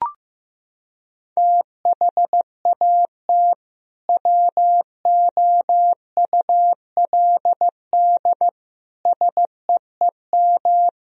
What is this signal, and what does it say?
Morse code, heard as a keyed tone, about 15 wpm, 700 Hz. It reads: THAT WOULD SEEM
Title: Sentences from Top 1000 Words - Review Part 1 - 15wpm